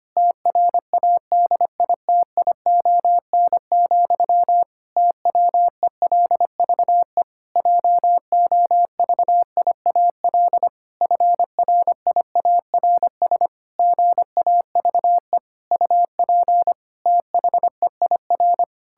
Morse code, heard as a keyed tone, about 25 wpm, 700 Hz. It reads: TRADITION, TWELVE JOVIAL FRIARS GAVE UP THEIR
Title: Pangram Challenge v2 - 25wpm